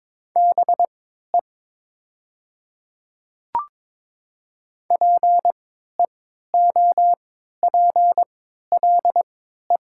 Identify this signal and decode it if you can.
Morse code, heard as a keyed tone, about 22 wpm, spaced as for 12 wpm, 700 Hz. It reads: BE PEOPLE
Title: 3x Character Spacing - Top 100 Words - 22wpm